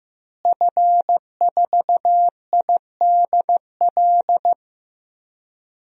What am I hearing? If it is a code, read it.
Morse code, heard as a keyed tone, about 15 wpm, 700 Hz. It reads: F4IDL
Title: DXCC-100 Call Signs - 5 Characters - Repeated 2x - 15wpm